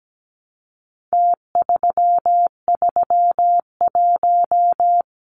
Morse code, heard as a keyed tone, about 17 wpm, 700 Hz. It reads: T331
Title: Numbers - Contest Style - 4-Digit 17wpm